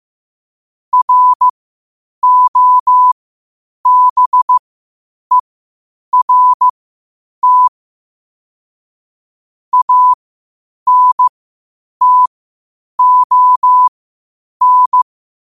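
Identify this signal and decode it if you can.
Morse code, heard as a keyed tone, about 15 wpm, spaced as for 8 wpm, 1000 Hz. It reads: ROBERT ANTON
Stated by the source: Morse code at 8 wpm.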